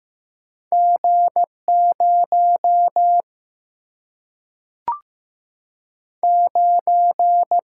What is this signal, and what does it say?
Morse code, heard as a keyed tone, about 15 wpm, 700 Hz. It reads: G 0 9
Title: Single-Letter Single-Number 15wpm